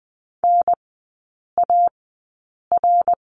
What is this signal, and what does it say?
Morse code, heard as a keyed tone, about 20 wpm, spaced as for 8 wpm, 700 Hz. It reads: NAR
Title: 138 Lesson 23e - Letter f - Three Characters 20wpm